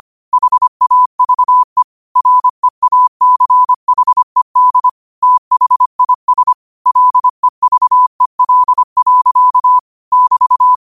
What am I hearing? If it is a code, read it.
Morse code, heard as a keyed tone, about 25 wpm, 1000 Hz. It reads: HAVE REACHED THIS LEVEL. <BT>
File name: qotdfiles / 25wpm